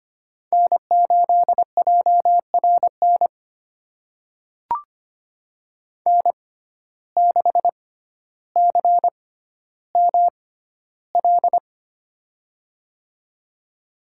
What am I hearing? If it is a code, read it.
Morse code, heard as a keyed tone, about 25 wpm, spaced as for 13 wpm, 700 Hz. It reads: N8JRN N6CML
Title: USA Call Signs - 5 Characters - 6x Character Spacing - 25wpm